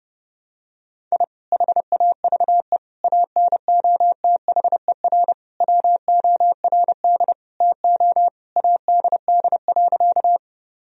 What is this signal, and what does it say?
Morse code, heard as a keyed tone, about 30 wpm, 700 Hz. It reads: I HAVE ANOTHER WORD TO ADD.
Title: Sentences from Top 1000 Words - Review Part 1 - 30wpm